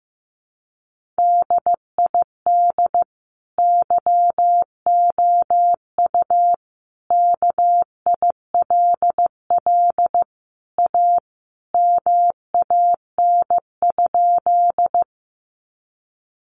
Morse code, heard as a keyed tone, about 15 wpm, 700 Hz. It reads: DID YOU KILL A MAN?
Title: Sentences from Top 1400 Words 15wpm